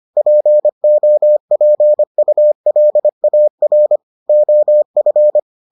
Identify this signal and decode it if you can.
Morse code, heard as a keyed tone, about 25 wpm, 600 Hz. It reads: POPULAR OF